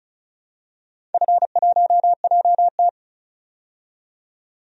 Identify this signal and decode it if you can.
Morse code, heard as a keyed tone, about 36 wpm, 700 Hz. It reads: F1JT
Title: DXCC-100 Call Signs - 4 Characters - 35wpm